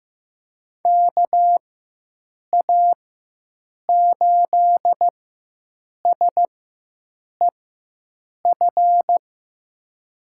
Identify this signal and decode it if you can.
Morse code, heard as a keyed tone, about 15 wpm, spaced as for 7 wpm, 700 Hz. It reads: KA8SEF
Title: USA Call Signs - 6 Characters - 4x Character Spacing - 15wpm